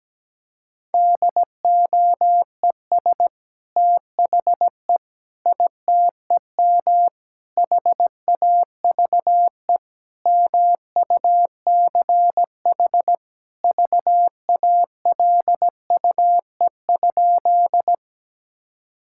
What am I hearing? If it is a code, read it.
Morse code, heard as a keyed tone, about 17 wpm, 700 Hz. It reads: DOES THE ITEM HAVE MUCH VALUE?
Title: Sentences from Top 1000 Words - Review Part 6 - 17wpm